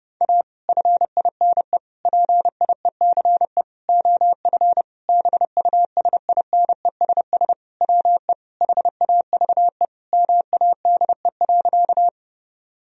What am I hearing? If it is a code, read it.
Morse code, heard as a keyed tone, about 30 wpm, 700 Hz. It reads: A FINE PIECE OF BUSINESS WE HAVE MADE.
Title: Sentences from Top 1000 Words - Review Part 2 - 30wpm